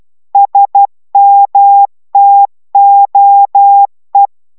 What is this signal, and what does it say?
Morse code, heard as a keyed tone, about 12 wpm, 800 Hz. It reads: SMTOE